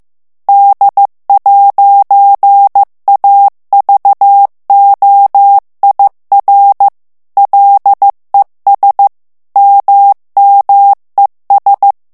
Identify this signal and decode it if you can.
Morse code, heard as a keyed tone, about 15 wpm, 800 Hz. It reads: D'AVOIR LES MMES
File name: morse mp3 / FRANCAIS TEXTE CLAIR 2 VITESSE 14 MOTS MINUTES